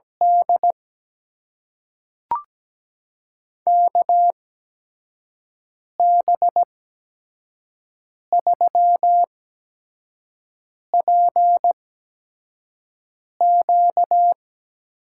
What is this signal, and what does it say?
Morse code, heard as a keyed tone, about 17 wpm, spaced as for 5 wpm, 700 Hz. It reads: D KB3PQ
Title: USA Call Signs - 5 Characters - 8x Character Spacing - 17wpm